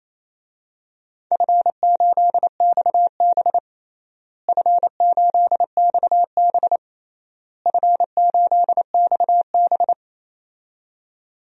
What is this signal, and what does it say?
Morse code, heard as a keyed tone, about 28 wpm, 700 Hz. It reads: F8XB F8XB F8XB
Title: DXCC-100 Call Signs - 4 Characters - Repeated 3x - 28wpm